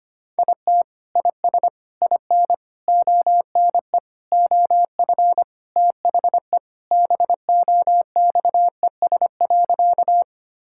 Morse code, heard as a keyed tone, about 25 wpm, 700 Hz. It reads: IT IS IN ONE OF THE BOXES.
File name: practice / 25wpm / Sentences with 3-Word Phrases - 25wpm